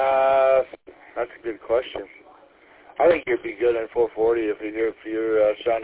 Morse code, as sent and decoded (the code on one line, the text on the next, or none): none